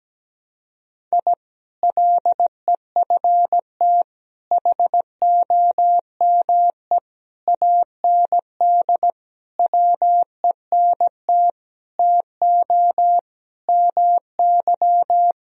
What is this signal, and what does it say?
Morse code, heard as a keyed tone, about 17 wpm, 700 Hz. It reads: I LEFT HOME AND WENT TO MY
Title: Sentences from Top 1000 Words - Review Part 3 - 17wpm